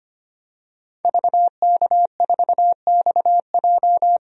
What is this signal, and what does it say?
Morse code, heard as a keyed tone, about 25 wpm, 700 Hz. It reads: VK4XJ
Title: DXCC-100 Call Signs - 5 Characters - 25wpm